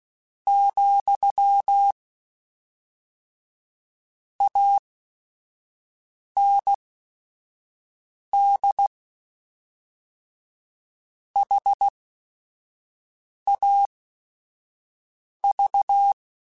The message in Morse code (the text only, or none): , AND HAV